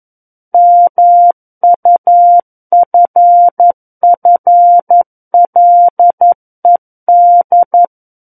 Morse code, heard as a keyed tone, about 11 wpm, 700 Hz. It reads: MUFFLED
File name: Book One / 02-Chapter 2 - 11 WPM